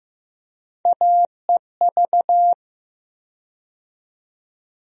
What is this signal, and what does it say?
Morse code, heard as a keyed tone, about 15 wpm, 700 Hz. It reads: AEV
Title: USA Call Sign Postfixes - 3 Letters 15wpm